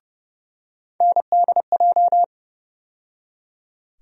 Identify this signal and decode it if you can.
Morse code, heard as a keyed tone, about 31 wpm, 700 Hz. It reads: NDJ
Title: USA Call Sign Postfixes - 3 Letters 30wpm